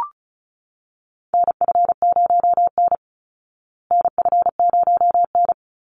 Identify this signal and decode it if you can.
Morse code, heard as a keyed tone, about 34 wpm, 700 Hz. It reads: NF0N NF0N
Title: Call Signs Repeated 3x - 35wpm